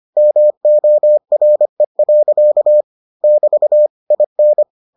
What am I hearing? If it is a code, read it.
Morse code, heard as a keyed tone, about 25 wpm, 600 Hz. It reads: MORE. <BT> IN